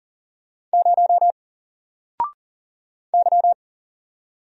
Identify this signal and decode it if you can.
Morse code, heard as a keyed tone, about 39 wpm, 700 Hz. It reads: 0 Y
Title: Single Character (Letter:Number:Punctuation:Prosign) 40wpm